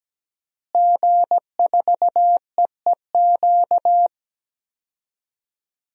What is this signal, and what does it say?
Morse code, heard as a keyed tone, about 17 wpm, 700 Hz. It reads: G4EEQ
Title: DXCC-100 Call Signs - 5 Characters - Repeated 2x - 17wpm